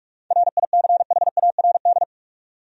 Morse code, heard as a keyed tone, about 46 wpm, 700 Hz. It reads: RICHARD
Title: USA Common Names - 45wpm